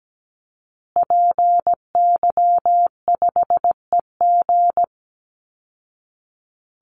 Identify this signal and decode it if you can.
Morse code, heard as a keyed tone, about 17 wpm, 700 Hz. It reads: PY5EG
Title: Call Signs Repeated 2x - 17wpm